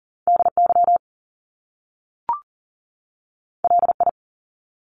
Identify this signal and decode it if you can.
Morse code, heard as a keyed tone, about 40 wpm, 700 Hz. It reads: DY LI
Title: Top Two Letter Combinations in Words 40wpm